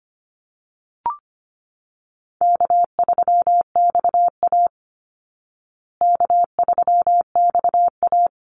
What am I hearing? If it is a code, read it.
Morse code, heard as a keyed tone, about 25 wpm, 700 Hz. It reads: K3XA K3XA
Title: Call Signs Repeated 3x - 25wpm